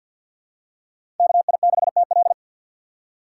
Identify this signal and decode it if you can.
Morse code, heard as a keyed tone, about 49 wpm, 700 Hz. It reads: KI6TL
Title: USA Call Signs - 6x Character Spacing - 50wpm